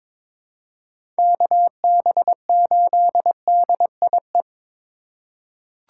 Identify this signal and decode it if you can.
Morse code, heard as a keyed tone, about 22 wpm, 700 Hz. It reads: KB8DIE
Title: USA Call Signs - 4x Character Spacing - 22wpm